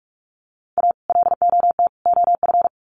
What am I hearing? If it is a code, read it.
Morse code, heard as a keyed tone, about 44 wpm, 700 Hz. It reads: A LOT OF
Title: Sets of 3 Words 45wpm